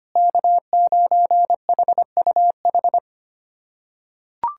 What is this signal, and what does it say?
Morse code, heard as a keyed tone, about 25 wpm, 700 Hz. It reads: K9HUH
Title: USA Callsign DE Callsign - 5 Characters - 25wpm